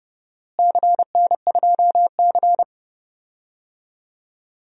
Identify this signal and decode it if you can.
Morse code, heard as a keyed tone, about 30 wpm, 700 Hz. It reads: CN2C
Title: DXCC-100 Call Signs - 4 Characters - Repeated 3x - 30wpm